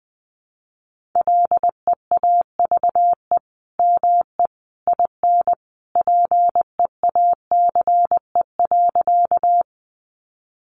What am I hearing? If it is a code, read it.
Morse code, heard as a keyed tone, about 20 wpm, 700 Hz. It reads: LEAVE ME IN PEACE.
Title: Sentences from Top 1500 Words 20wpm